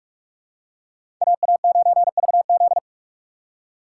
Morse code, heard as a keyed tone, about 44 wpm, 700 Hz. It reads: AA9VZ
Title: USA Call Signs - 5 Characters - 6x Character Spacing - 45wpm